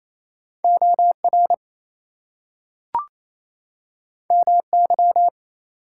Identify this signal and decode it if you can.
Morse code, heard as a keyed tone, about 28 wpm, 700 Hz. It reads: OR MY
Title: Two Letter Words - 28wpm